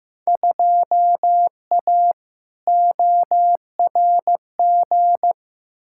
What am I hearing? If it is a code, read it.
Morse code, heard as a keyed tone, about 15 wpm, 700 Hz. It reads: 2A ORG
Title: Field Day Exchanges Encore - 15wpm